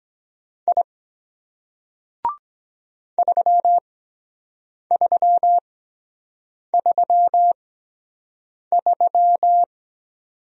Morse code, heard as a keyed tone, about 23 wpm, spaced as for 7 wpm, 700 Hz. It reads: I 3333